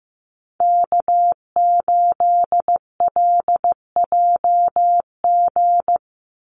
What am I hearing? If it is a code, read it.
Morse code, heard as a keyed tone, about 15 wpm, 700 Hz. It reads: K8LJG
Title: NA SOTA Chaser Callsigns 15wpm